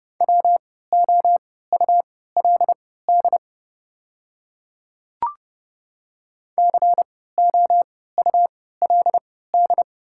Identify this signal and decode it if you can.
Morse code, heard as a keyed tone, about 30 wpm, spaced as for 17 wpm, 690 Hz. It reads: WOULD COULD
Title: 3x Character Spacing - Top 300 Words - 30wpm